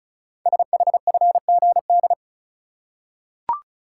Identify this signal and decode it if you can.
Morse code, heard as a keyed tone, about 36 wpm, 700 Hz. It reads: SHFGD
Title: Code Groups - 5 Letters - 35wpm